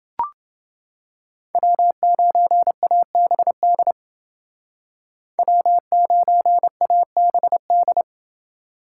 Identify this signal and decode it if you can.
Morse code, heard as a keyed tone, about 28 wpm, 700 Hz. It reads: W9ABD W9ABD